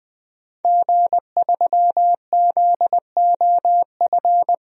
Morse code, heard as a keyed tone, about 19 wpm, 700 Hz. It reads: G3ZOF
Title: DXCC-100 Call Signs - 5 Characters - Repeated 2x - 20wpm